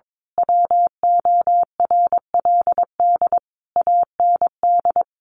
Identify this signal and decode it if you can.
Morse code, heard as a keyed tone, about 22 wpm, 700 Hz. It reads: WORLD AND
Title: Sentences from Top 1300 Words 22wpm